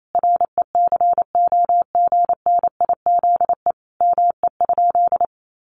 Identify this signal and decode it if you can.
Morse code, heard as a keyed tone, about 28 wpm, 700 Hz. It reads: RECOGNIZE ME?
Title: Sentences from Top 1400 Words 28wpm